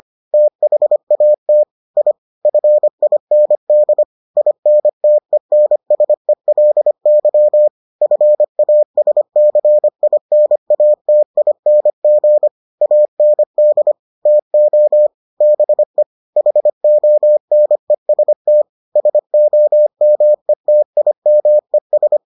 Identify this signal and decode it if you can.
Morse code, heard as a keyed tone, about 25 wpm, 600 Hz. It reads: THAT I FIND INTENSELY FASCINATING AND TO BE HONEST SOMETIMES